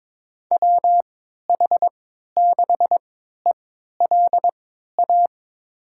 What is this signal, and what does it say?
Morse code, heard as a keyed tone, about 23 wpm, spaced as for 13 wpm, 700 Hz. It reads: WH6ELA
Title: USA Call Signs - 6 Characters - 3x Character Spacing - 22wpm